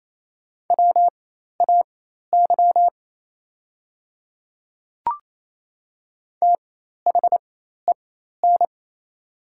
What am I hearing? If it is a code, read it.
Morse code, heard as a keyed tone, about 28 wpm, spaced as for 13 wpm, 700 Hz. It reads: WAY THEN
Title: 4x Character Spacing - Top 100 Words - 28wpm